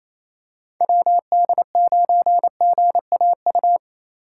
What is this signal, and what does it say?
Morse code, heard as a keyed tone, about 28 wpm, 700 Hz. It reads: WD9GAU
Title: USA Call Signs - 6 Characters - 3x Character Spacing - 28wpm